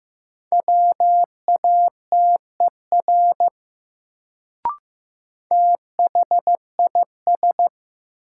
Morse code, heard as a keyed tone, about 15 wpm, 700 Hz. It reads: WATER THIS
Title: Sentences from Top 100 Words - New Word Drill - 15wpm